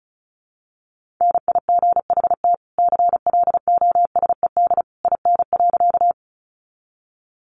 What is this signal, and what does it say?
Morse code, heard as a keyed tone, about 34 wpm, 700 Hz. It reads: NIGHT CLOSED IN.
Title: Sentences from Top 1400 Words 35wpm